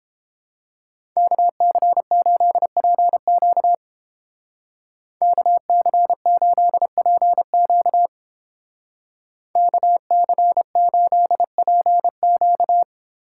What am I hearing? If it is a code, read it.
Morse code, heard as a keyed tone, about 30 wpm, 700 Hz. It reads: KC8PQ KC8PQ KC8PQ